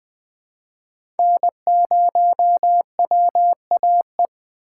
Morse code, heard as a keyed tone, about 20 wpm, 700 Hz. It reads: N0WAE
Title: 2023 SOTA Chaser Call Signs - 20wpm